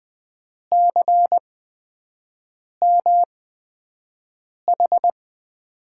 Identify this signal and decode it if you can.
Morse code, heard as a keyed tone, about 20 wpm, spaced as for 5 wpm, 700 Hz. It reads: CMH
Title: USA Call Signs - 8x Character Spacing - 20wpm